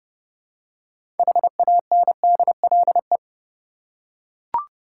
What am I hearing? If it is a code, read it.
Morse code, heard as a keyed tone, about 32 wpm, 700 Hz. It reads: HANDLE
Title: Six Letter Words - 30wpm